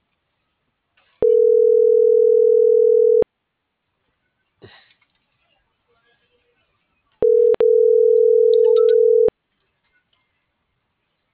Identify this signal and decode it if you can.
Ringback tone (North America), 440 + 480 Hz.
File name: CENTRAL SAGOT / out anyelo1.sagot - .11389